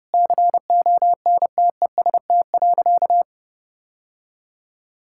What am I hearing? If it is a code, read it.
Morse code, heard as a keyed tone, about 32 wpm, 700 Hz. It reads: CONTEST.
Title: Sentences from Top 1600 Words 30wpm